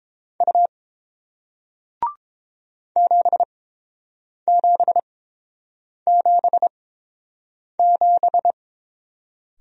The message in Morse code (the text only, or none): U 7777